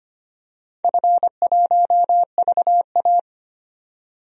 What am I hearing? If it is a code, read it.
Morse code, heard as a keyed tone, about 25 wpm, 700 Hz. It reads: F1VA
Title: DXCC-100 Call Signs - 4 Characters - 25wpm